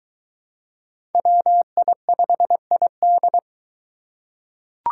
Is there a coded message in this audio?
Yes: Morse code from 1.1 to 3.4 s, heard as a keyed tone, about 20 wpm, 700 Hz. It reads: WI5ID